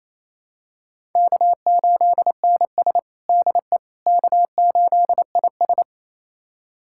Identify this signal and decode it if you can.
Morse code, heard as a keyed tone, about 28 wpm, 700 Hz. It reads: K8NS DE K8IS
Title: USA Callsign DE Callsign - 4 Characters - 28wpm